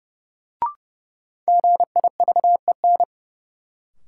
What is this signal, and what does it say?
Morse code, heard as a keyed tone, about 30 wpm, 700 Hz. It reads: GIVEN
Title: Sentences from Top 1400 Words - New Word Drill - 30wpm